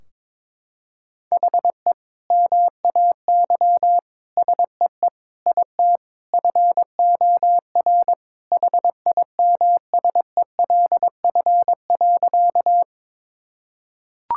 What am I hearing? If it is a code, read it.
Morse code, heard as a keyed tone, about 21 wpm, 700 Hz. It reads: HE MAY SEE IT FOR HIMSELF.
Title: Sentences from Top 1600 Words 22wpm